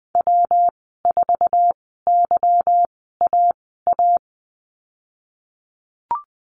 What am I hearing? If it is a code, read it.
Morse code, heard as a keyed tone, about 20 wpm, spaced as for 14 wpm, 700 Hz. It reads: W4YAA
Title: 147 Lesson 24g - Letter y - Call Signs 20wpm